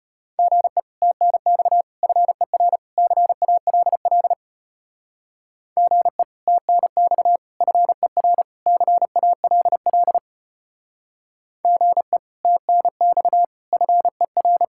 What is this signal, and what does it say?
Morse code, heard as a keyed tone, about 30 wpm, 700 Hz. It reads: GE TNX FER CALL GE TNX FER CALL GE TNX FER
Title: Speed-Racing Common QSO Phrases Encore - 25wpm